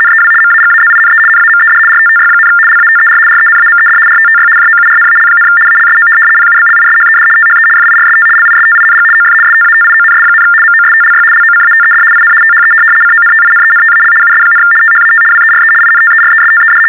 RTTY, about 75 baud. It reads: NEW MAGAZINE NAMED RTTY, LATER RENAMED RTTY JOURNAL, ALSO PUBLISHED THE FIRST LISTING OF STATIONS, MOSTLY LOCATED IN THE CONTINENTAL US, THAT WERE INTERESTED
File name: teletype-RTTY-75